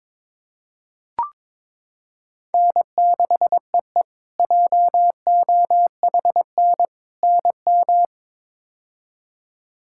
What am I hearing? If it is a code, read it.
Morse code, heard as a keyed tone, about 21 wpm, 700 Hz. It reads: N6EE JOHN NM
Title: K1USN SST QSO Practice - 22wpm